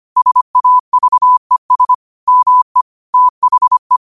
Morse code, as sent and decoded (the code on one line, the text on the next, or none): ... .- ...- . ... / -- . / - .... .
SAVES ME THE